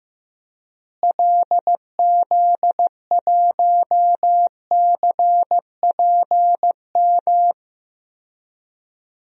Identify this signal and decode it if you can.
Morse code, heard as a keyed tone, about 15 wpm, 700 Hz. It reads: LZ1CPM
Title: DXCC-100 Call Signs - 6 Characters - Repeated 3x - 15wpm